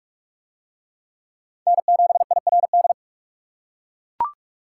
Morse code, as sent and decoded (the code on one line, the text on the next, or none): -. --... .. .-. -..
N7IRD